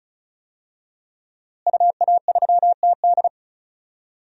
Morse code, heard as a keyed tone, about 35 wpm, 700 Hz. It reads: UA3TD
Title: DXCC-100 Call Signs - 5 Characters - Repeated 3x - 35wpm